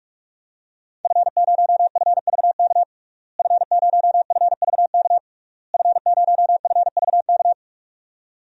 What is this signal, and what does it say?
Morse code, heard as a keyed tone, about 46 wpm, 700 Hz. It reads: F0FVK F0FVK F0FVK
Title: DXCC-100 Call Signs - 5 Characters - Repeated 3x - 45wpm